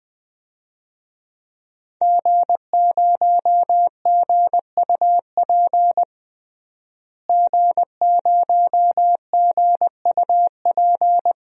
Morse code, heard as a keyed tone, about 21 wpm, 700 Hz. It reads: G0GUP G0GUP
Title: DXCC-100 Call Signs - 5 Characters - Repeated 3x - 20wpm